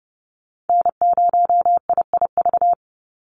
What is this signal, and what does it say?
Morse code, heard as a keyed tone, about 30 wpm, 700 Hz. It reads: N0IIV
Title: USA Call Signs 30wpm